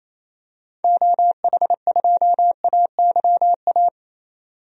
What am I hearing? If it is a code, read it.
Morse code, heard as a keyed tone, about 28 wpm, 700 Hz. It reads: OH2AYA
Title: DXCC-100 Call Signs - 6 Characters - 28wpm